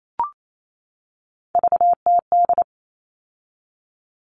Morse code, heard as a keyed tone, about 28 wpm, 700 Hz. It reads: VTD
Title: All Cut Numbers - 3-Digit 28wpm